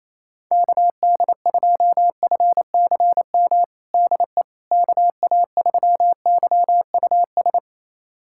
Morse code, heard as a keyed tone, about 27 wpm, 700 Hz. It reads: KD2FCM DE KA3YUS
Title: USA Callsign DE Callsign - 6 Characters - 28wpm